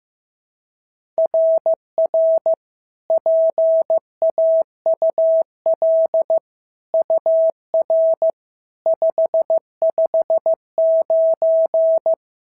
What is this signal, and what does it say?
Morse code, heard as a keyed tone, about 14 wpm, 650 Hz. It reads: RR PAUL UR 559
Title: SOTA QSO Practice 15wpm